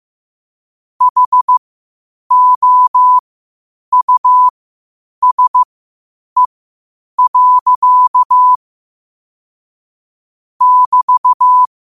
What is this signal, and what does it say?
Morse code, heard as a keyed tone, about 15 wpm, spaced as for 8 wpm, 1000 Hz. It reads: HOUSE. <BT>
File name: qotdfiles / 8wpm